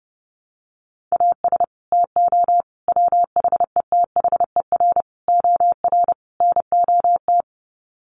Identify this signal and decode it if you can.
Morse code, heard as a keyed tone, about 30 wpm, 700 Hz. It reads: AS TO WHETHER OR NOT
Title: Sets of 5 Words Encore 30wpm